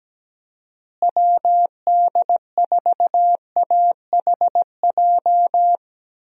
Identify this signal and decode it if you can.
Morse code, heard as a keyed tone, about 17 wpm, 700 Hz. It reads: WD4AHJ
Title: USA Call Signs - 6 Characters - Repeated 2x - 17wpm